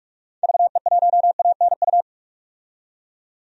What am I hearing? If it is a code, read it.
Morse code, heard as a keyed tone, about 46 wpm, 700 Hz. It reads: VE1ANU
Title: DXCC-100 Call Signs - 6 Characters - 45wpm